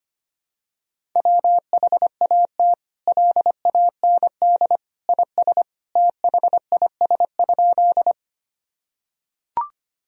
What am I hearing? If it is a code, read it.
Morse code, heard as a keyed tone, about 25 wpm, 700 Hz. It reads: WHAT LAND IS THIS?
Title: Sentences from Top 1000 Words - Review Part 1 - 25wpm